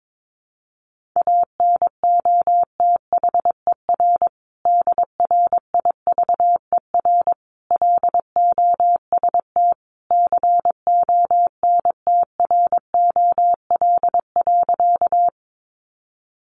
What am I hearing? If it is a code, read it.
Morse code, heard as a keyed tone, about 22 wpm, 700 Hz. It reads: ANOTHER DRIVER LOST CONTROL.
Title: Sentences from Top 1400 Words 22wpm